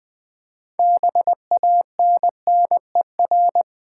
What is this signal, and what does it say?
Morse code, heard as a keyed tone, about 20 wpm, 700 Hz. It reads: BANNER
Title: 6 Letter Words - First 3 Letters Given - 20wpm